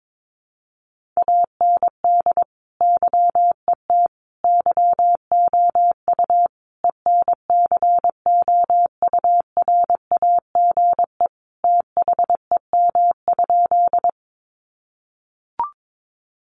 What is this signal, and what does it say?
Morse code, heard as a keyed tone, about 22 wpm, 700 Hz. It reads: AND YET YOU ENCOURAGE THEM?
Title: Sentences from Top 1400 Words 22wpm